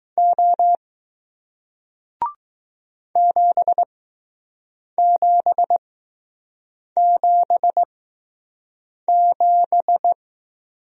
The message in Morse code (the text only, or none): O 7777